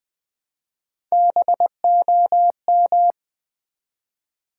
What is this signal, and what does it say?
Morse code, heard as a keyed tone, about 20 wpm, 700 Hz. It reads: BOM
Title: Build A Word - 1 Letter At A Time - 4-Letter Words - 20wpm